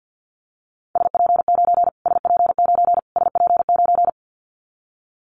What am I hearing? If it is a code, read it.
Morse code, heard as a keyed tone, about 49 wpm, 700 Hz. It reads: SP9 SP9 SP9
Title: Callsign Prefixes for Top 25 DXCC - Repeated 3x - 50wpm